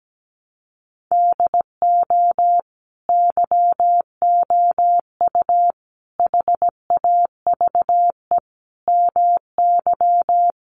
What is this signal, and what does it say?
Morse code, heard as a keyed tone, about 17 wpm, 700 Hz. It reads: DO YOU HAVE MY
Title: Sentences from Top 500 Words 17wpm